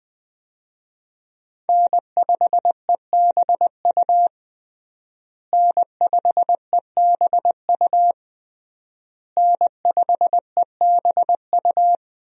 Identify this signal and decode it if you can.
Morse code, heard as a keyed tone, about 21 wpm, 700 Hz. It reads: N5EBU N5EBU N5EBU
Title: USA Call Signs - 5 Characters - Repeated 3x - 20wpm